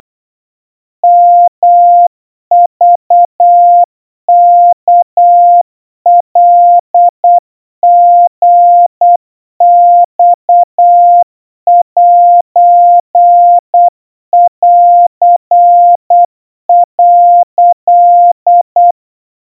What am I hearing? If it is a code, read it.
Morse code (Japanese Wabun code), heard as a keyed tone, about 8 wpm, 700 Hz. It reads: ヨクワカリマセン」
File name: 30 cpm 700 Hz Crispy